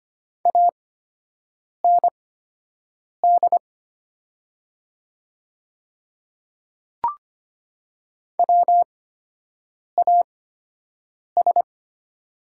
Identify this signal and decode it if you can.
Morse code, heard as a keyed tone, about 25 wpm, spaced as for 7 wpm, 700 Hz. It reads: AND WAS
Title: 8x Character Spacing - Top 200 Words - 25wpm